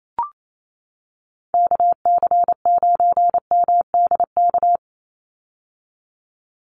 Morse code, heard as a keyed tone, about 28 wpm, 700 Hz. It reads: KC9MDK
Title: USA Call Signs 28wpm